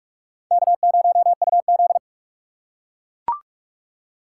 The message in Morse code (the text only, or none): K0UZ